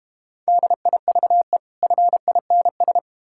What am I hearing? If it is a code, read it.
Morse code, heard as a keyed tone, about 32 wpm, 700 Hz. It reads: DIVE FINS